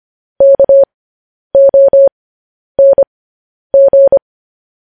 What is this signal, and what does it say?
Morse code, heard as a keyed tone, about 25 wpm, spaced as for 10 wpm, 550 Hz. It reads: KONG
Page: News Headlines in Morse Code at 10 WPM